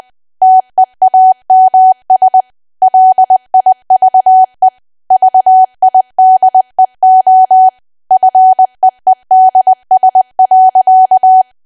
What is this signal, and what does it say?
Morse code, heard as a keyed tone, about 20 wpm, 750 Hz. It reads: TEAMS LIVE VIDEO FEEDS.
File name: Archive / 20 WPM / 20WPM